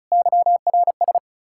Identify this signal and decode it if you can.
Morse code, heard as a keyed tone, about 44 wpm, 700 Hz. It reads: YRS
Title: Speed-Racing Common QSO Phrases - 35wpm